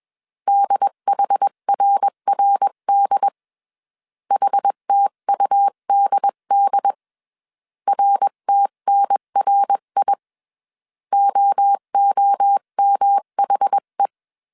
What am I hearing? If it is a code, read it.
Morse code, heard as a keyed tone, about 21 wpm, 800 Hz. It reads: DHRRD HTUDD RTNRI OOMHE
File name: 20wpm